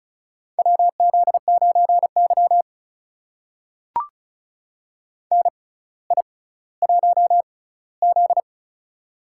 Morse code, heard as a keyed tone, about 36 wpm, spaced as for 18 wpm, 700 Hz. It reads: WZ9Y NI1Z
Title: USA Call Signs - 4 Characters - 6x Character Spacing - 35wpm